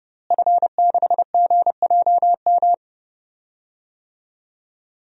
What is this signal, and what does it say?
Morse code, heard as a keyed tone, about 30 wpm, 700 Hz. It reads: F6GJM
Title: DXCC-100 Call Signs - 5 Characters - Repeated 3x - 30wpm